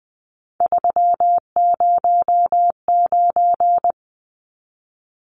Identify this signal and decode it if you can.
Morse code, heard as a keyed tone, about 20 wpm, 700 Hz. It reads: 309
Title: Numbers - 3-Digit 20wpm